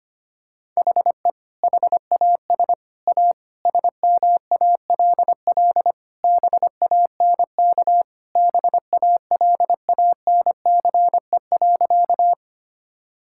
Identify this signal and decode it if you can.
Morse code, heard as a keyed tone, about 25 wpm, 700 Hz. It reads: HE HAS A SMALL BANK BALANCE.
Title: Sentences from Top 1000 Words - Review Part 6 - 25wpm